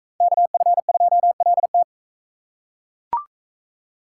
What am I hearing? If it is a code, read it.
Morse code, heard as a keyed tone, about 39 wpm, 700 Hz. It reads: KF2LT